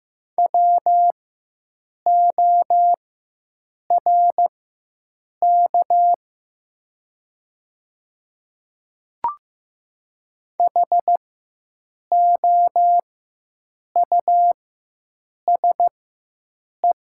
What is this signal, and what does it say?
Morse code, heard as a keyed tone, about 15 wpm, spaced as for 7 wpm, 700 Hz. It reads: WORK HOUSE
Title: 4x Character Spacing - Top 200 Words - 15wpm